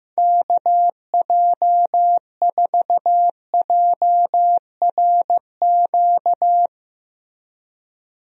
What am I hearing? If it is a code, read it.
Morse code, heard as a keyed tone, about 15 wpm, 700 Hz. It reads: KJ4JRQ
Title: USA Call Signs - 6 Characters - Repeated 2x - 15wpm